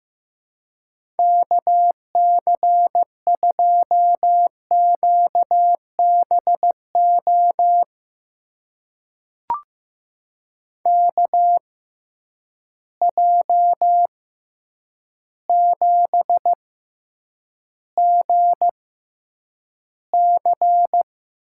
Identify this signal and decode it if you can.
Morse code, heard as a keyed tone, about 15 wpm, spaced as for 8 wpm, 700 Hz. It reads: KC2QBO KJ7GC
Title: USA Call Signs - 6 Characters - 6x Character Spacing - 15wpm